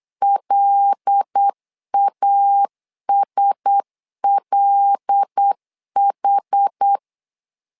Morse code, heard as a keyed tone, about 8 wpm, 800 Hz. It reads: LASLH
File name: LEZIONI / 5 LBVC / 5 L 8wpm 2831